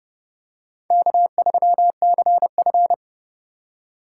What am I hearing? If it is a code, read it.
Morse code, heard as a keyed tone, about 30 wpm, 700 Hz. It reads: K3CF
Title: USA Call Signs - 4 Characters - 3x Character Spacing - 30wpm